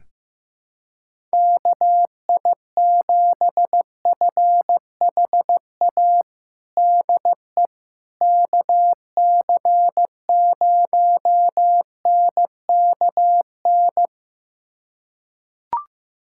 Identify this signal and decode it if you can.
Morse code, heard as a keyed tone, about 15 wpm, 700 Hz. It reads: KI7FHA DE KC0NKN
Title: USA Callsign DE Callsign 15wpm